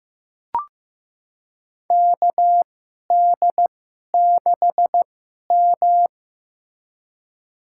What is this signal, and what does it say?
Morse code, heard as a keyed tone, about 15 wpm, spaced as for 11 wpm, 700 Hz. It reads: KD6M
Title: USA Call Signs - 4 Characters - 2x Character Spacing - 15wpm